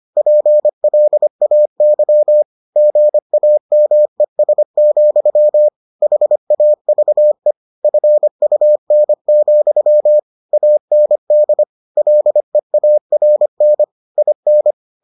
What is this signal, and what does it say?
Morse code, heard as a keyed tone, about 25 wpm, 600 Hz. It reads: PLAY GAMES, HAVE FUN, AND LEARN IN